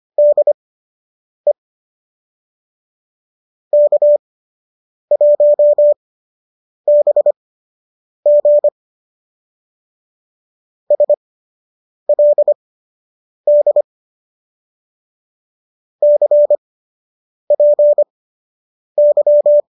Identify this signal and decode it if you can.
Morse code, heard as a keyed tone, about 25 wpm, spaced as for 8 wpm, 590 Hz. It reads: DE K1BG SLD CPY